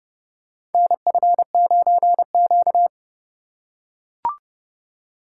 Morse code, heard as a keyed tone, about 30 wpm, 700 Hz. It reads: NF9Q
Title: USA Call Signs - 4 Characters - 3x Character Spacing - 30wpm